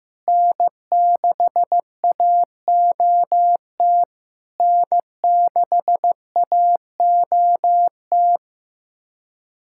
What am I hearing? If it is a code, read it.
Morse code, heard as a keyed tone, about 15 wpm, 700 Hz. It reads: N6AOT N6AOT
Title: SOTA QSO Practice 15wpm